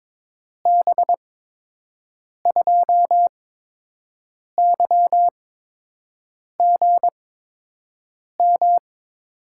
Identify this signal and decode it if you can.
Morse code, heard as a keyed tone, about 22 wpm, spaced as for 6 wpm, 700 Hz. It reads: B2YGM
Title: USA Call Signs - 6 Characters - 8x Character Spacing - 22wpm